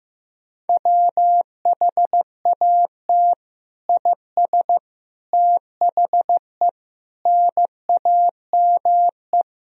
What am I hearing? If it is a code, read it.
Morse code, heard as a keyed tone, about 15 wpm, 700 Hz. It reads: WHAT IS THE NAME
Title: Sentences from Top 1000 Words - Review Part 3 - 15wpm